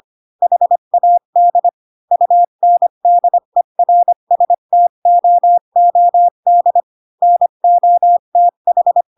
Morse code, heard as a keyed tone, about 25 wpm, 700 Hz. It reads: HAD UNDERSTOOD NOTH
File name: Pinocchio / Pinocchio ch7 20 25 ik1hge